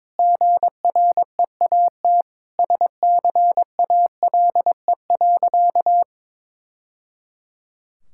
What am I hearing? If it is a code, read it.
Morse code, heard as a keyed tone, about 22 wpm, 700 Hz. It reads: GREAT SCALE.
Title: Sentences from Top 1000 Words - Review Part 5 - 22wpm